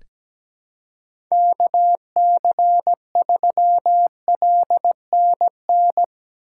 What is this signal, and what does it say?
Morse code, heard as a keyed tone, about 17 wpm, 700 Hz. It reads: KC3LNN
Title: USA Call Signs - 2x Character Spacing - 17wpm